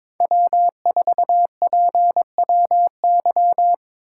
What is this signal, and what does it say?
Morse code, heard as a keyed tone, about 22 wpm, 700 Hz. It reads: W4PWY
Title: USA Call Signs - 5 Characters - 4x Character Spacing - 22wpm